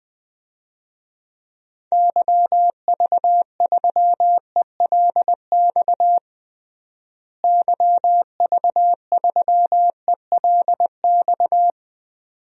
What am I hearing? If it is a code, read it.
Morse code, heard as a keyed tone, about 20 wpm, 700 Hz. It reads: YV3ELX YV3ELX
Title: DXCC-100 Call Signs - 6 Characters - Repeated 2x - 20wpm